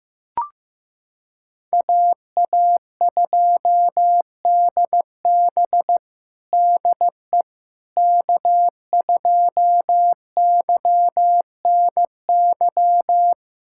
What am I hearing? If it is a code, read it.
Morse code, heard as a keyed tone, about 16 wpm, 700 Hz. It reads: AA2DB DE K2YNY
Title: USA Callsign DE Callsign - 5 Characters - 15wpm